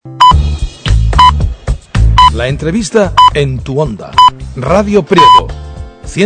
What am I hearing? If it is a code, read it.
Radio time signal (time pips), with speech. It is heard from 0.2 to 5.4 s.